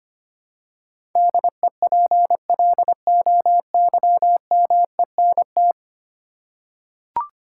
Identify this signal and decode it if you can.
Morse code, heard as a keyed tone, about 25 wpm, 700 Hz. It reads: DEPLOYMENT
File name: Ten Letter Words - 25wpm